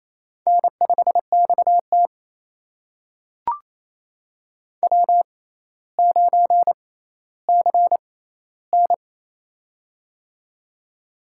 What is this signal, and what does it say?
Morse code, heard as a keyed tone, about 28 wpm, spaced as for 14 wpm, 700 Hz. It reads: N5XT W9CN
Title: USA Call Signs - 4 Characters - 6x Character Spacing - 28wpm